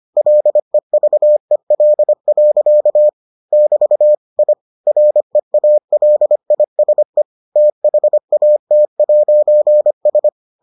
Morse code, heard as a keyed tone, about 25 wpm, 600 Hz. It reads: LEVEL. <BT> I REALISE THAT'S